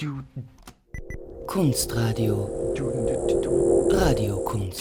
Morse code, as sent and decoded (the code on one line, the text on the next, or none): none